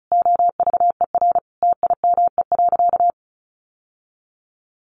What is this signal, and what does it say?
Morse code, heard as a keyed tone, about 34 wpm, 700 Hz. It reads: OVER TIME.
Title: Sentences from Top 1200 Words 35wpm